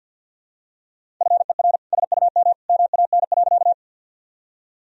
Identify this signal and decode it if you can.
Morse code, heard as a keyed tone, about 50 wpm, 700 Hz. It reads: FER SUM MAN.
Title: Sentences from Top 200 Words and Abbreviations 50wpm